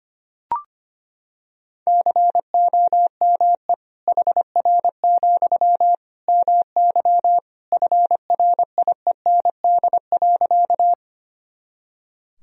Morse code, heard as a keyed tone, about 25 wpm, 700 Hz. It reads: COME HR, MY FRIEND.
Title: Sentences from Top 300 Words and Abbreviations 25wpm